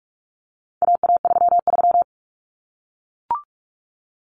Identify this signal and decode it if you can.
Morse code, heard as a keyed tone, about 44 wpm, 700 Hz. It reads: AA33
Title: Common Cut Numbers - 4-Digit 45wpm